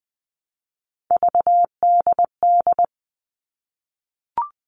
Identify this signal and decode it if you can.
Morse code, heard as a keyed tone, about 20 wpm, 700 Hz. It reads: VDD
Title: All Cut Numbers - 3-Digit 20wpm